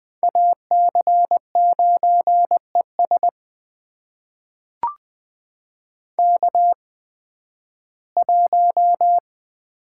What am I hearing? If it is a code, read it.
Morse code, heard as a keyed tone, about 20 wpm, spaced as for 13 wpm, 700 Hz. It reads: AC9ES K1
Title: USA Call Signs - 5 Characters - 8x Character Spacing - 20wpm